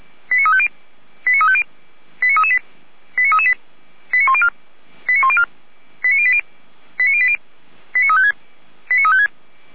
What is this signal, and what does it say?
Five-tone selective call (ZVEI), heard as quick runs of five tones, tones of about 70 ms. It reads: 89350 89350 89338 89338 89104 89104 89080 89080 89356 89356